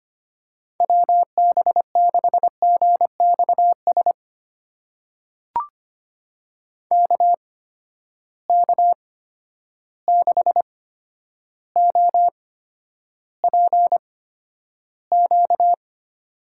Morse code, heard as a keyed tone, about 25 wpm, spaced as for 11 wpm, 700 Hz. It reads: WB6GXS KK6OPQ
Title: USA Call Signs - 6 Characters - 8x Character Spacing - 25wpm